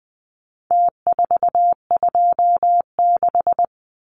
Morse code, heard as a keyed tone, about 20 wpm, 700 Hz. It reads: T426
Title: Numbers - Contest Style - 4-Digit 20wpm